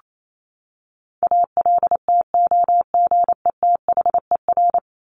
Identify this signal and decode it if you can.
Morse code, heard as a keyed tone, about 28 wpm, 700 Hz. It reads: ALTOGETHER
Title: Top 1501-1750 Words 28wpm